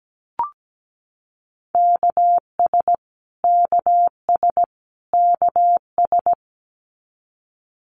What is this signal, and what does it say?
Morse code, heard as a keyed tone, about 17 wpm, 700 Hz. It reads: KS KS KS
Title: US States Repeated 3x 17wpm